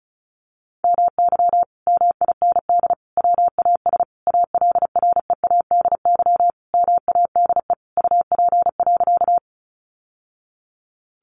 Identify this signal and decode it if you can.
Morse code, heard as a keyed tone, about 34 wpm, 700 Hz. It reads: MY MIND WAS ALREADY MADE UP.
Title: Sentences from Top 700 Words 35wpm